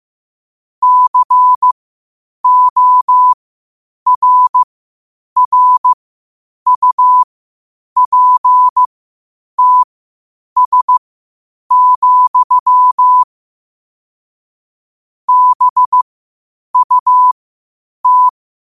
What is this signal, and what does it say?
Morse code, heard as a keyed tone, about 15 wpm, spaced as for 8 wpm, 1000 Hz. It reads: CORRUPTS, BUT
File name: qotdfiles / 8wpm